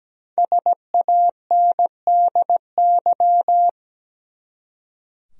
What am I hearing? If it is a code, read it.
Morse code, heard as a keyed tone, about 17 wpm, 700 Hz. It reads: SANDY
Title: USA Common Names - 17wpm